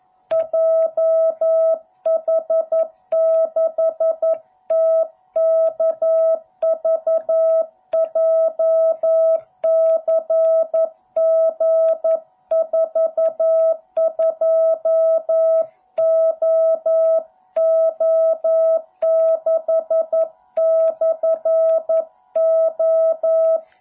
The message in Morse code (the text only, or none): JH6TKV JCG42OO6/O